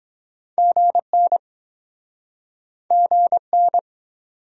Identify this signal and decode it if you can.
Morse code, heard as a keyed tone, about 26 wpm, 700 Hz. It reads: GN GN